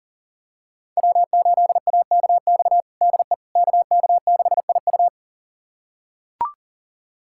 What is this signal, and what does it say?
Morse code, heard as a keyed tone, about 40 wpm, 700 Hz. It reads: W8AKX DE KK6IU
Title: USA Callsign DE Callsign - 5 Characters - 40wpm